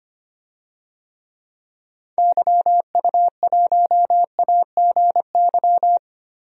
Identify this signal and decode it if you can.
Morse code, heard as a keyed tone, about 25 wpm, 700 Hz. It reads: YU1AGY
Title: DXCC-100 Call Signs - 6 Characters - 25wpm